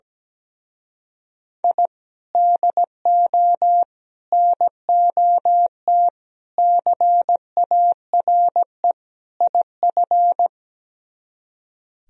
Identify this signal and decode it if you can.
Morse code, heard as a keyed tone, about 18 wpm, 700 Hz. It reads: I DO NOT CARE IF
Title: Speed-Racing - Sets of 5 Words 17wpm